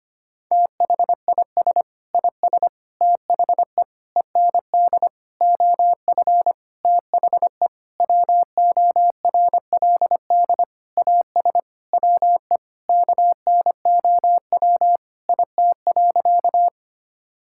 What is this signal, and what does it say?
Morse code, heard as a keyed tone, about 25 wpm, 700 Hz. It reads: THIS IS THE END OF THE WORLD AS WE KNOW IT.
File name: practice / 25wpm / Sentences with 9-Word Phrases - 25wpm